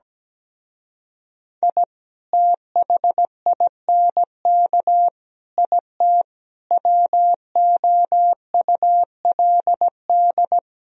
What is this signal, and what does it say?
Morse code, heard as a keyed tone, about 17 wpm, 700 Hz. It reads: I THINK IT WOULD
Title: Speed-Racing - Sets of 5 Words 17wpm